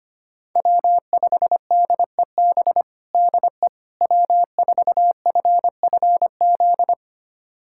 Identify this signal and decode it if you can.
Morse code, heard as a keyed tone, about 25 wpm, 700 Hz. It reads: W5DEB DE W4FFZ
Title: USA Callsign DE Callsign - 5 Characters - 25wpm